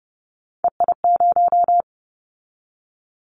Morse code, heard as a keyed tone, about 30 wpm, 700 Hz. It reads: EI0
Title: Callsign Prefixes for Top 50 DXCC - 30wpm